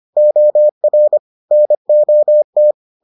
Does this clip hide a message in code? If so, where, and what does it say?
Yes: Morse code from 0.2 to 2.7 s, heard as a keyed tone, about 25 wpm, 600 Hz. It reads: OR NOT